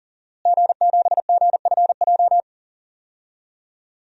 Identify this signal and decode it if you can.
Morse code, heard as a keyed tone, about 40 wpm, 700 Hz. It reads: G7GFJ
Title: DXCC-100 Call Signs - 5 Characters - Repeated 3x - 40wpm